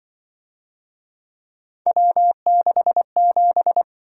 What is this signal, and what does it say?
Morse code, heard as a keyed tone, about 24 wpm, 700 Hz. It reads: W67
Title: Speed-Racing - Three Letter-Number Combinations - 20wpm